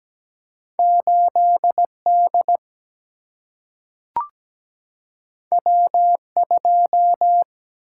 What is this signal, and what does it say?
Morse code, heard as a keyed tone, about 17 wpm, 700 Hz. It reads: 8D W2
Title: Single-Letter Single-Number 17wpm